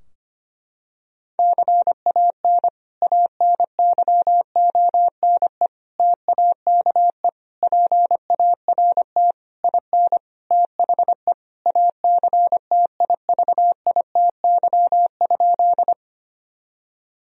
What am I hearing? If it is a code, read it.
Morse code, heard as a keyed tone, about 25 wpm, 700 Hz. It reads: CAN ANYONE TAKE PART IN THE ACTIVITY?
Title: Sentences from Top 1000 Words - Review Part 4 - 25wpm